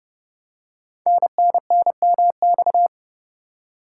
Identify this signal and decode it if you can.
Morse code, heard as a keyed tone, about 30 wpm, 700 Hz. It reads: NNNMX
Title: Code Groups - 5 Letters - 30wpm